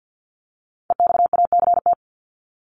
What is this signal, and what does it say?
Morse code, heard as a keyed tone, about 53 wpm, 700 Hz. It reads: EXACT
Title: Top 1251-1500 Words 50wpm